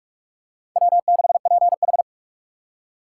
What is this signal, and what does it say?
Morse code, heard as a keyed tone, about 46 wpm, 700 Hz. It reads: W6PH
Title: USA Call Signs - 2x Character Spacing - 45wpm